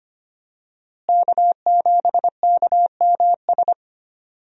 Morse code, heard as a keyed tone, about 25 wpm, 700 Hz. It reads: K7KMS
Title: USA Call Signs - 5 Characters - Repeated 3x - 25wpm